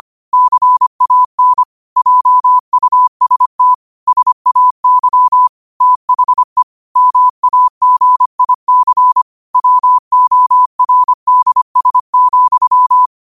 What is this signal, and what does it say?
Morse code, heard as a keyed tone, about 25 wpm, 1000 Hz. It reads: CAN JUST SAY THE MAGIC WORDS,
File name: qotdfiles / 25wpm